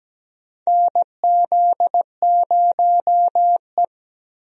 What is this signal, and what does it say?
Morse code, heard as a keyed tone, about 17 wpm, 700 Hz. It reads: NZ0E
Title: USA Call Signs - 4 Characters - 17wpm